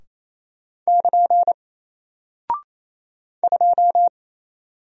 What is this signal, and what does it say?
Morse code, heard as a keyed tone, about 27 wpm, 700 Hz. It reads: <KN> 2
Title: Single Character (Letter:Number:Punctuation:Prosign) 28wpm